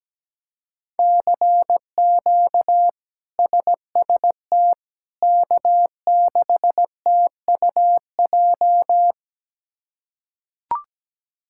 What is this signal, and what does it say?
Morse code, heard as a keyed tone, about 17 wpm, 700 Hz. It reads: CQ SST K6TUJ
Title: K1USN SST QSO Practice - 17wpm